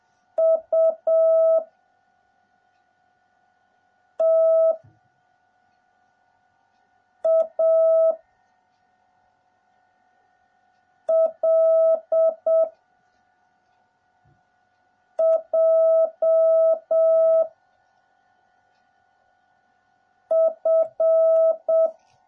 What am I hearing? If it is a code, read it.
Morse code, heard as a keyed tone, about 7 wpm, 660 Hz. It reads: U T A L J F